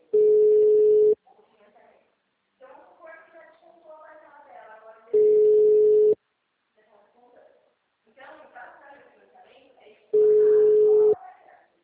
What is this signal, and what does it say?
Ringback tone (Europe, 425 Hz), with speech.
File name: out- -2006- - - .569